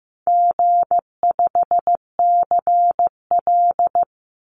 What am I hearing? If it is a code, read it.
Morse code, heard as a keyed tone, about 15 wpm, 700 Hz. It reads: G5CL
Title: Easy Call Signs Repeated 3x - 15wpm